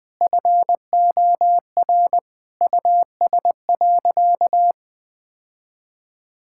Morse code, heard as a keyed tone, about 20 wpm, 700 Hz. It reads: FOR US.
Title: Sentences from Top 1000 Words - Review Part 5 - 20wpm